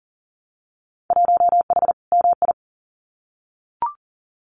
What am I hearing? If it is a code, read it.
Morse code, heard as a keyed tone, about 42 wpm, 700 Hz. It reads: 1H MI
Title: Winter Field Day Exchanges 40wpm